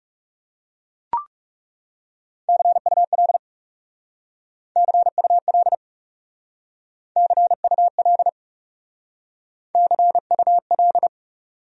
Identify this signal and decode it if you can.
Morse code, heard as a keyed tone, about 36 wpm, 700 Hz. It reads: CUL CUL CUL CUL
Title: Speed-Racing Abbreviations - 30wpm